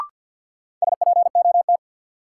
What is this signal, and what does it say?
Morse code, heard as a keyed tone, about 52 wpm, 700 Hz. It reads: SPOT
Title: Sentences from Top 1100 Words - New Word Drill - 50wpm